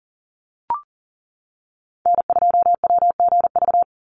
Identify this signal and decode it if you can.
Morse code, heard as a keyed tone, about 40 wpm, 700 Hz. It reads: N2WGV
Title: USA Call Signs 40wpm